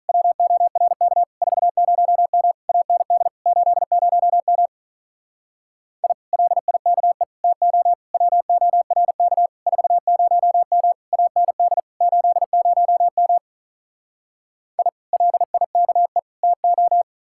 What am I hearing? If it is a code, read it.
Morse code, heard as a keyed tone, about 47 wpm, 700 Hz. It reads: WORK 40M AND 80M I LIKE TO WORK 40M AND 80M I LIKE TO